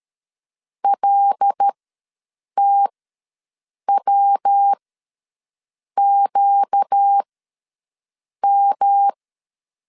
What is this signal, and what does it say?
Morse code, heard as a keyed tone, about 13 wpm, spaced as for 6 wpm, 800 Hz. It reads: LTWQM
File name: LEZIONI / 6 PKX0 / 6 PK ws 12wpm 3116